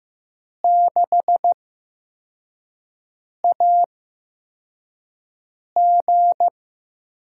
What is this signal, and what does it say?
Morse code, heard as a keyed tone, about 15 wpm, spaced as for 4 wpm, 700 Hz. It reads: 6AG
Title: USA Call Signs - 4 Characters - 8x Character Spacing - 15wpm